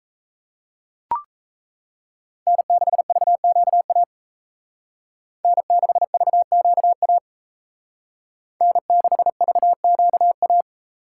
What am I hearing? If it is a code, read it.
Morse code, heard as a keyed tone, about 36 wpm, 700 Hz. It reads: N6VQA N6VQA N6VQA